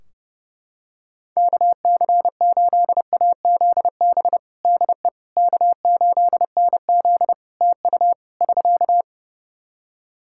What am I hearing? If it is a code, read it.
Morse code, heard as a keyed tone, about 31 wpm, 700 Hz. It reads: KC8AZB DE K8NZ TU <SK>
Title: Practice QSO - Exchange 4 - Ending - 30wpm